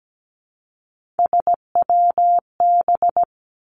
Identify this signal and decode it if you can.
Morse code, heard as a keyed tone, about 17 wpm, 700 Hz. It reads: SWB
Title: Three Challenging Letter-Number Combinations 17wpm